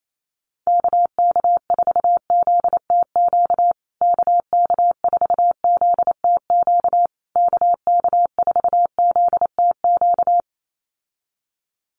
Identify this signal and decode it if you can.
Morse code, heard as a keyed tone, about 28 wpm, 700 Hz. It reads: KK4ZTQ KK4ZTQ KK4ZTQ
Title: USA Call Signs Repeated 3x 28wpm